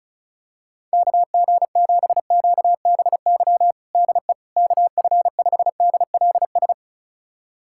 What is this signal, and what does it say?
Morse code, heard as a keyed tone, about 36 wpm, 700 Hz. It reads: KG7QBY DE KF5DLS
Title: USA Callsign DE Callsign - 6 Characters - 35wpm